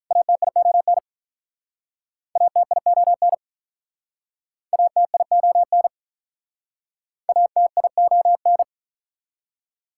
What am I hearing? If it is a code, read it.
Morse code, heard as a keyed tone, about 46 wpm, 700 Hz. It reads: ATION ATION ATION ATION